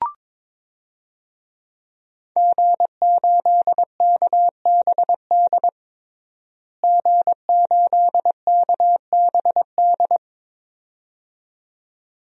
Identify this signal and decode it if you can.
Morse code, heard as a keyed tone, about 22 wpm, 700 Hz. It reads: G8KBD G8KBD
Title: DXCC-100 Call Signs - 5 Characters - Repeated 2x - 22wpm